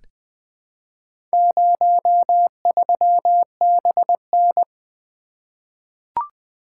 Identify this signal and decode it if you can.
Morse code, heard as a keyed tone, about 20 wpm, 700 Hz. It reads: 03BN
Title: Four Letter-Number Combinations - 20wpm